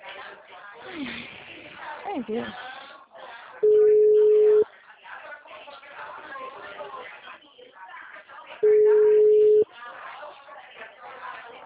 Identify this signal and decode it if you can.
Ringback tone (Europe, 425 Hz), with speech.